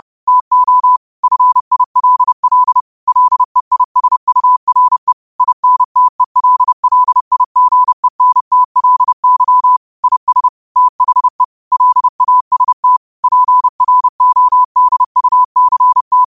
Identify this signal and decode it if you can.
Morse code, heard as a keyed tone, about 30 wpm, 1000 Hz. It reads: TO FILL LEISURE INTELLIGENTLY IS THE LAST PRODUCT
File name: qotdfiles / 30wpm